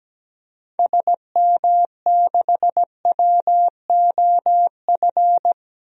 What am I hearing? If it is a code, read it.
Morse code, heard as a keyed tone, about 17 wpm, 700 Hz. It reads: SM6WOF
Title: DXCC-100 Call Signs - 6 Characters - 17wpm